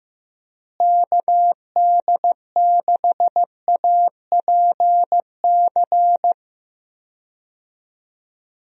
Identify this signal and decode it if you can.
Morse code, heard as a keyed tone, about 16 wpm, 700 Hz. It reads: KD6APC
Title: USA Call Signs - 6 Characters - Repeated 2x - 15wpm